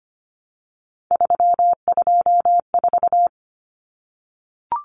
Morse code, heard as a keyed tone, about 25 wpm, 700 Hz. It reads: 324
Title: Numbers - 3-Digit 25wpm